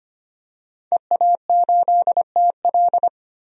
Morse code, heard as a keyed tone, about 25 wpm, 700 Hz. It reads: EA8TL
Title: DXCC-100 Call Signs - 5 Characters - 25wpm